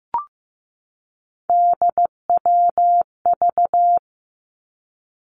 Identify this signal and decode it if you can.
Morse code, heard as a keyed tone, about 15 wpm, 700 Hz. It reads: DWV
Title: Three Challenging Letter-Number Combinations 15wpm